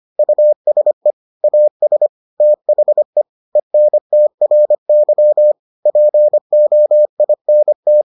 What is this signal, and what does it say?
Morse code, heard as a keyed tone, about 25 wpm, 600 Hz. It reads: USE AS THE ENTRY POINT